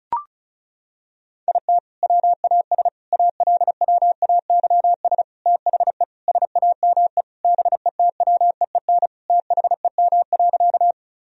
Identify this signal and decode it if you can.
Morse code, heard as a keyed tone, about 36 wpm, 700 Hz. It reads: IT WAS ALWAYS THE SAME BETWEEN THEM.
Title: Sentences from Top 1000 Words - Review Part 1 - 35wpm